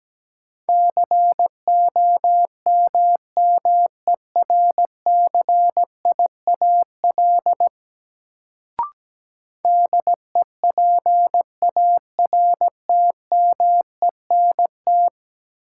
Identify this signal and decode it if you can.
Morse code, heard as a keyed tone, about 17 wpm, 700 Hz. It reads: COMMERCIAL DEPARTMENT
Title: Sentences from Top 1000 Words - Review Part 5 - New Word Drill - 17wpm